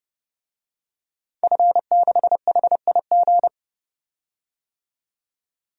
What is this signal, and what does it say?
Morse code, heard as a keyed tone, about 32 wpm, 700 Hz. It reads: F6HIG
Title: DXCC-100 Call Signs - 5 Characters - 30wpm